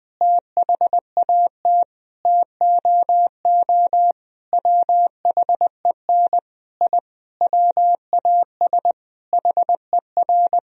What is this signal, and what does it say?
Morse code, heard as a keyed tone, about 20 wpm, 700 Hz. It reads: THAT TOO WHEN I WAS HER
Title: Sentences from Top 1000 Words - Review Part 3 - 20wpm